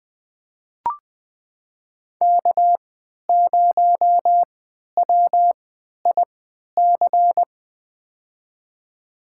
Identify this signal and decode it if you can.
Morse code, heard as a keyed tone, about 20 wpm, spaced as for 11 wpm, 700 Hz. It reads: K0WIC
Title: USA Call Signs - 5 Characters - 3x Character Spacing - 20wpm